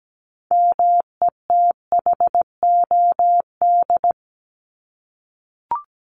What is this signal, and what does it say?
Morse code, heard as a keyed tone, about 17 wpm, 700 Hz. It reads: METHOD
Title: ES:S Suffix - Top 100 Words with Base Words 17wpm